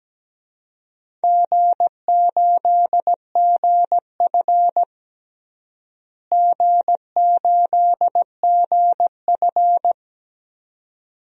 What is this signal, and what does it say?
Morse code, heard as a keyed tone, about 17 wpm, 700 Hz. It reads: G8GF G8GF
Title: DXCC-100 Call Signs - 4 Characters - Repeated 2x - 17wpm